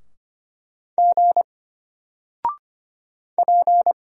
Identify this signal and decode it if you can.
Morse code, heard as a keyed tone, about 25 wpm, 700 Hz. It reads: G P
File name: Single-Letter-Number-25wpm